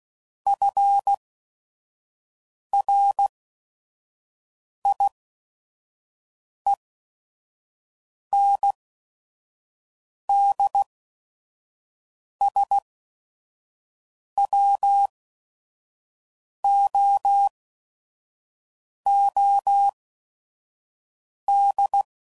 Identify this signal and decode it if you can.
Morse code, heard as a keyed tone, about 16 wpm, spaced as for 5 wpm, 800 Hz. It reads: FRIENDSWOOD
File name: CW Code Groups @5WPM / fromSM's / Cities